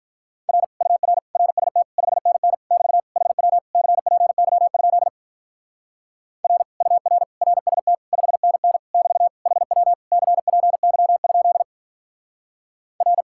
Morse code, heard as a keyed tone, about 43 wpm, 700 Hz. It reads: R UR RST 5NN <BT> HW CPY? R UR RST 5NN <BT> HW CPY? R
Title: Speed-Racing Common QSO Phrases Encore - 35wpm